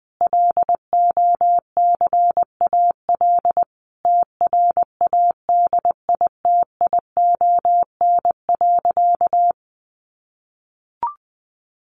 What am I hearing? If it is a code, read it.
Morse code, heard as a keyed tone, about 20 wpm, 700 Hz. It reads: LOCAL TRADITION.
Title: Sentences from Top 1500 Words 20wpm